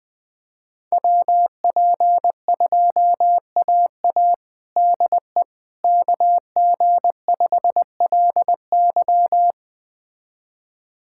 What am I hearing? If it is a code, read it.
Morse code, heard as a keyed tone, about 20 wpm, 700 Hz. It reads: WP2AA DE KG5LY
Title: USA Callsign DE Callsign - 5 Characters - 20wpm